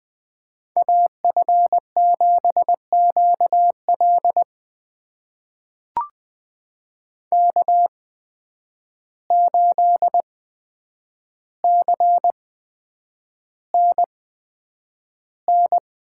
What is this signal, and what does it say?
Morse code, heard as a keyed tone, about 20 wpm, spaced as for 9 wpm, 700 Hz. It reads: AF7QL K8CNN
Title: USA Call Signs - 5 Characters - 8x Character Spacing - 20wpm